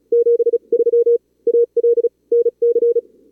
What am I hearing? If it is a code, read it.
Morse code, heard as a keyed tone, about 42 wpm, spaced as for 34 wpm, 440 Hz. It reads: 73 AL NC
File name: CWSprint / Feb2022 / k5tr-feb-na-sprint-cw-k5ot -2300